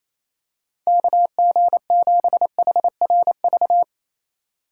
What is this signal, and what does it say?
Morse code, heard as a keyed tone, about 28 wpm, 700 Hz. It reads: KG7HRV
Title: USA Call Signs - 4x Character Spacing - 28wpm